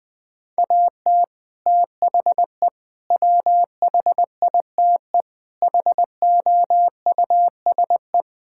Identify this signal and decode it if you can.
Morse code, heard as a keyed tone, about 20 wpm, 700 Hz. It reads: AT THE WHITE HOUSE
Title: Build A Phrase - 4-Word Phrases - 20wpm